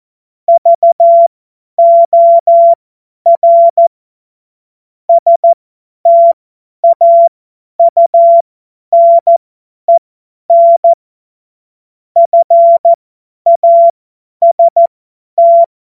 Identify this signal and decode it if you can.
Morse code, heard as a keyed tone, about 14 wpm, spaced as for 10 wpm, 680 Hz. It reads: VOR STAUNEN FAST
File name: Morsekurs / Witze / Witz 19 März 2024 50 F70